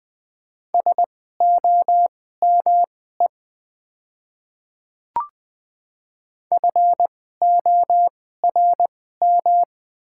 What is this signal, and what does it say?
Morse code, heard as a keyed tone, about 20 wpm, spaced as for 15 wpm, 700 Hz. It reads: SOME FORM
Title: 2x Character Spacing - Top 300 Words - 20wpm